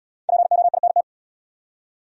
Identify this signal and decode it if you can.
Morse code, heard as a keyed tone, about 71 wpm, 700 Hz. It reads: POINT